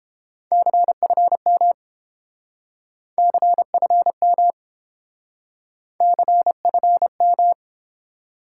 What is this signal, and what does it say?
Morse code, heard as a keyed tone, about 30 wpm, 700 Hz. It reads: CFM CFM CFM